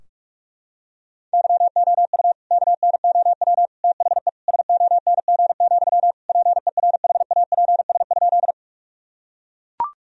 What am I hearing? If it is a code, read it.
Morse code, heard as a keyed tone, about 41 wpm, 700 Hz. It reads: YOU KNOW THE SONG, PERHAPS?
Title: Sentences from Top 1000 Words - Review Part 2 - 45wpm